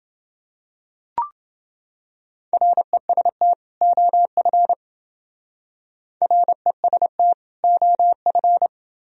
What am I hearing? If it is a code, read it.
Morse code, heard as a keyed tone, about 28 wpm, 700 Hz. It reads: REST OF REST OF